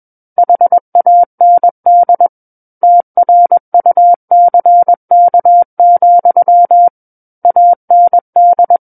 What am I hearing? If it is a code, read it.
Morse code, heard as a keyed tone, about 21 wpm, 700 Hz. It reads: HAND TRUCK, AND
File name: Book One / 12-Chapter 12 - 21 WPM